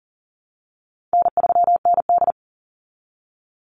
Morse code, heard as a keyed tone, about 40 wpm, 700 Hz. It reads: N3ND
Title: Easy Call Signs Repeated 3x - 40wpm